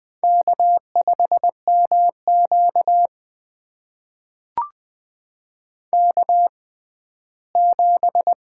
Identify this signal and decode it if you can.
Morse code, heard as a keyed tone, about 20 wpm, spaced as for 14 wpm, 700 Hz. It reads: K5MQ K7
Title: USA Call Signs - 4 Characters - 6x Character Spacing - 20wpm